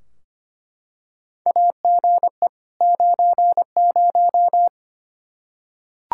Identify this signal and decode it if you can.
Morse code, heard as a keyed tone, about 25 wpm, 700 Hz. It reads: AGE 90
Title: Practice QSO - Exchange 3 - More Info - 25wpm